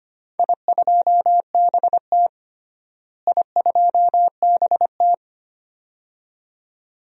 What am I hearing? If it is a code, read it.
Morse code, heard as a keyed tone, about 25 wpm, 700 Hz. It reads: I2BT I2BT
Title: DXCC-100 Call Signs - 4 Characters - Repeated 2x - 25wpm